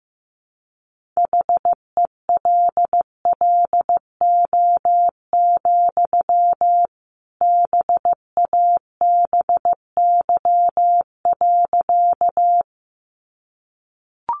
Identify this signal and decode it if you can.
Morse code, heard as a keyed tone, about 15 wpm, 700 Hz. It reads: HELLO, BABY.
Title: Sentences from Top 1300 Words 15wpm